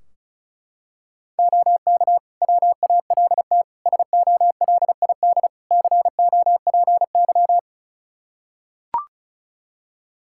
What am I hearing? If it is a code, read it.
Morse code, heard as a keyed tone, about 34 wpm, 700 Hz. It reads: OK WALT SOLID COPY
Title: Practice QSO - Exchange 3 - More Info - 35wpm